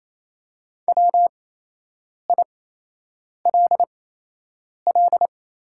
Morse code, heard as a keyed tone, about 28 wpm, spaced as for 8 wpm, 700 Hz. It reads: WILL
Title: 8x Character Spacing - Top 300 Words - 28wpm